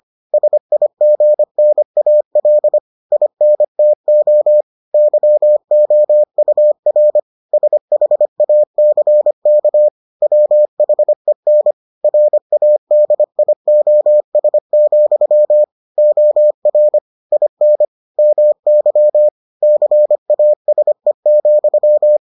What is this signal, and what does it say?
Morse code, heard as a keyed tone, about 25 wpm, 600 Hz. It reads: SIGNAL INTO YOUR SHACK WHEN RADIOS, OR IN MY CASE,